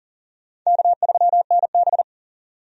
Morse code, heard as a keyed tone, about 40 wpm, 700 Hz. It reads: K3NB
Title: USA Call Signs - 4 Characters - 40wpm